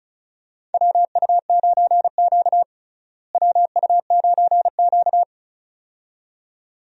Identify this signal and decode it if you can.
Morse code, heard as a keyed tone, about 36 wpm, 700 Hz. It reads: WU9Q WU9Q
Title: USA Call Signs - 4 Characters - Repeated 2x - 35wpm